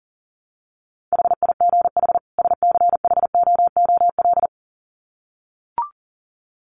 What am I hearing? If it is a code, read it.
Morse code, heard as a keyed tone, about 40 wpm, 700 Hz. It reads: HIGH SCHOOL
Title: Sets of 2 Words 40wpm